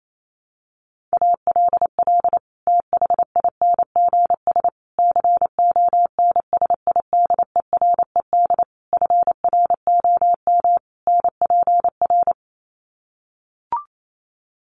Morse code, heard as a keyed tone, about 28 wpm, 700 Hz. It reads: ALL THINGS CONSIDERED FROM NPR
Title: Sets of 5 Words Encore 28wpm